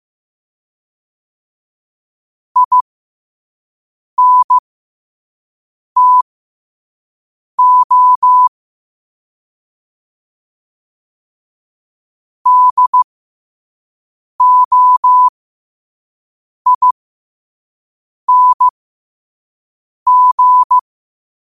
Morse code, heard as a keyed tone, about 15 wpm, spaced as for 5 wpm, 1000 Hz. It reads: INTO DOING
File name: qotdfiles / 5wpm